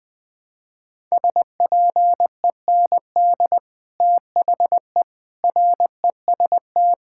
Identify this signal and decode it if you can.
Morse code, heard as a keyed tone, about 20 wpm, 700 Hz. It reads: SPEND THE REST
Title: Sets of 8 Words - 20wpm